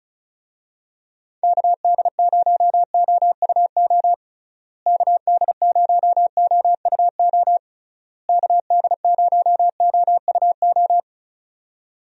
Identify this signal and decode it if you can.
Morse code, heard as a keyed tone, about 34 wpm, 700 Hz. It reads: KD0OUO KD0OUO KD0OUO
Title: USA Call Signs - 6 Characters - Repeated 3x - 35wpm